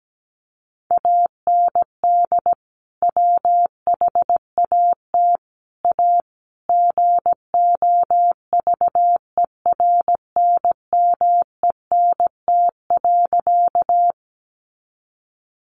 Morse code, heard as a keyed tone, about 17 wpm, 700 Hz. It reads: AND WHAT A GOVERNMENT.
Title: Sentences from Top 600 Words 17wpm